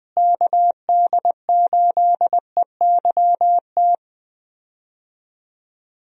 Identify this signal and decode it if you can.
Morse code, heard as a keyed tone, about 21 wpm, 700 Hz. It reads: KD8EYT
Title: USA Call Signs - 6 Characters - Repeated 3x - 20wpm